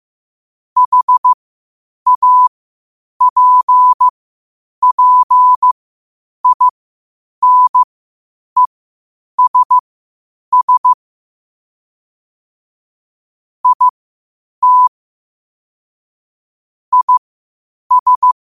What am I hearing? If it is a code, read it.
Morse code, heard as a keyed tone, about 15 wpm, spaced as for 8 wpm, 1000 Hz. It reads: HAPPINESS IT IS
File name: qotdfiles / 8wpm